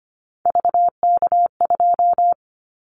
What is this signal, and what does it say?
Morse code, heard as a keyed tone, about 25 wpm, 700 Hz. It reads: VK2
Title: Callsign Prefixes for Top 100 DXCC - 25wpm